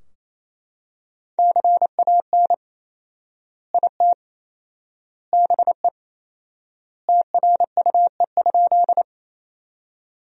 Morse code, heard as a keyed tone, about 28 wpm, 700 Hz. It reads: CAN IT BE TRUE?
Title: 4x Word-Spacing - Sentences from Top 400 Words - 28wpm